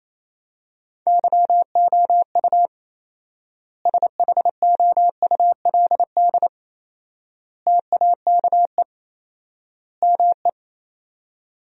Morse code, heard as a keyed tone, about 28 wpm, 700 Hz. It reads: YOU SHOULD TAKE ME
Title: 4x Word-Spacing - Sentences from Top 300 Words - 28wpm